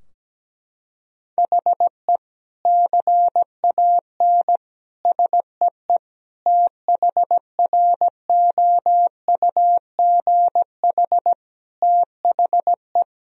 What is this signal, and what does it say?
Morse code, heard as a keyed tone, about 17 wpm, 700 Hz. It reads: HE CAN SEE THROUGH THE
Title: Sentences from Top 1000 Words - Review Part 2 - 17wpm